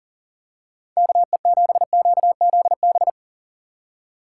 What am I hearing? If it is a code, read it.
Morse code, heard as a keyed tone, about 40 wpm, 700 Hz. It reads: KE7QZB
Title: USA Call Signs - 3x Character Spacing - 40wpm